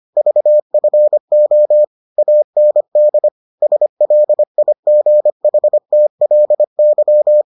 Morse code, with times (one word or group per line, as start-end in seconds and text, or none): 0.2-1.8 VFO
2.2-3.3 AND
3.6-7.4 SLIGHTLY